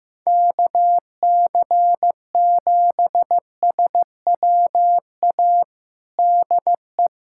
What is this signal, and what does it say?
Morse code, heard as a keyed tone, about 16 wpm, 700 Hz. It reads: KC7SWA DE
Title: Practice QSO - Exchange 4 - Ending - 15wpm